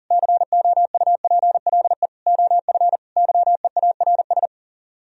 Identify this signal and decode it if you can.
Morse code, heard as a keyed tone, about 52 wpm, 700 Hz. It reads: COUPLE OF YEARS